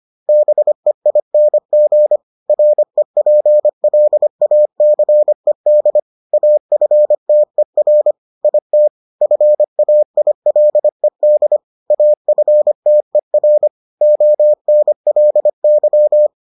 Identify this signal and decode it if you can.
Morse code, heard as a keyed tone, about 25 wpm, 600 Hz. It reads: BEING REPLACED AFTER IT FAILED AFTER ONLY